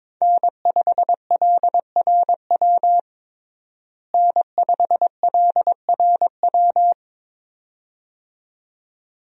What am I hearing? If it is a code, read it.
Morse code, heard as a keyed tone, about 23 wpm, 700 Hz. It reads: N5LRW N5LRW
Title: USA Call Signs - 5 Characters - Repeated 2x - 22wpm